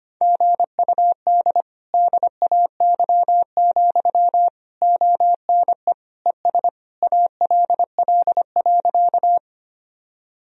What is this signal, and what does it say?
Morse code, heard as a keyed tone, about 25 wpm, 700 Hz. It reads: GUD DAY, ONE ES ALL.
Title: Sentences from Top 200 Words and Abbreviations 25wpm